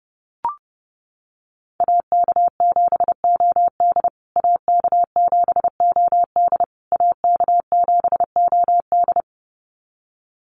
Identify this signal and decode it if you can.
Morse code, heard as a keyed tone, about 30 wpm, 700 Hz. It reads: AK7OD AK7OD AK7OD
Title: USA Call Signs Repeated 3x 30wpm